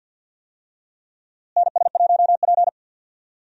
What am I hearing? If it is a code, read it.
Morse code, heard as a keyed tone, about 52 wpm, 700 Hz. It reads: NS1P
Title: USA Call Signs - 4 Characters - Repeated 3x - 50wpm